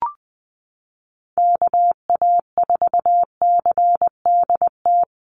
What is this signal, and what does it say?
Morse code, heard as a keyed tone, about 20 wpm, 700 Hz. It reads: KA4CDT
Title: USA Call Signs 20wpm